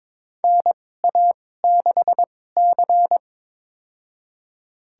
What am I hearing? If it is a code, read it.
Morse code, heard as a keyed tone, about 21 wpm, spaced as for 16 wpm, 700 Hz. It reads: NA6C
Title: USA Call Signs - 4 Characters - 2x Character Spacing - 22wpm